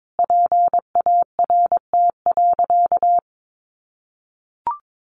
Morse code, heard as a keyed tone, about 22 wpm, 700 Hz. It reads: PART.
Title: Sentences from Top 1200 Words 22wpm